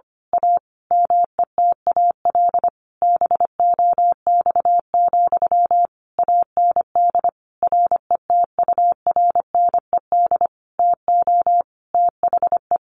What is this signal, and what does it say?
Morse code, heard as a keyed tone, about 25 wpm, 700 Hz. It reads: A METAL BOX, AND RETURNED TO THE
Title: A Sound Of Thunder 25wpm